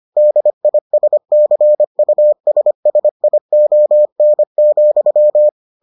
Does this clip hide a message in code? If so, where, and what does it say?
Yes: Morse code from 0.2 to 5.5 s, heard as a keyed tone, about 25 wpm, 600 Hz. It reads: DISCUSSION,